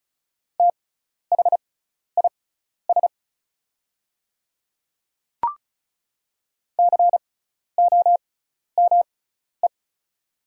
Morse code, heard as a keyed tone, about 36 wpm, spaced as for 12 wpm, 700 Hz. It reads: THIS COME
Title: 6x Character Spacing - Top 100 Words - 35wpm